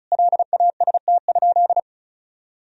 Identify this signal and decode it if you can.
Morse code, heard as a keyed tone, about 34 wpm, 700 Hz. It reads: LAST?
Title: Sentences from Top 1000 Words - Review Part 2 - 35wpm